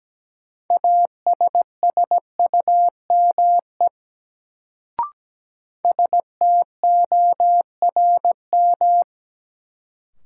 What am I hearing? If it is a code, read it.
Morse code, heard as a keyed tone, about 17 wpm, 700 Hz. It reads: ASSUME STORM
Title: Sentences from Top 1500 Words - New Word Drill Alt - 17wpm